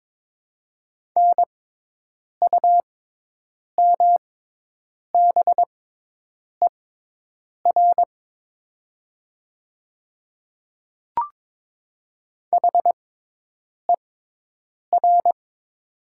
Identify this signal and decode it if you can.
Morse code, heard as a keyed tone, about 22 wpm, spaced as for 8 wpm, 700 Hz. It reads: NUMBER HER
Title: 6x Character Spacing - Top 100 Words - 22wpm